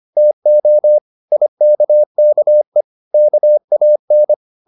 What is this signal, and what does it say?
Morse code, heard as a keyed tone, about 25 wpm, 600 Hz. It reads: TO IKKE KAN
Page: Ferie 25 wpm